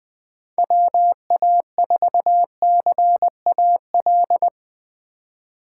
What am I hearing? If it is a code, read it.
Morse code, heard as a keyed tone, about 21 wpm, 700 Hz. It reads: WA4CAL
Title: Speed Racing - USA 2x3 Call Signs 20wpm